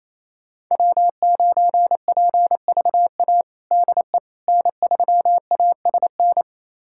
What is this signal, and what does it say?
Morse code, heard as a keyed tone, about 28 wpm, 700 Hz. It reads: W9PVA DE N3ASN
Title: USA Callsign DE Callsign - 5 Characters - 28wpm